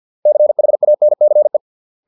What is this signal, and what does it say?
Morse code, heard as a keyed tone, about 50 wpm, 600 Hz. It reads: CHANCE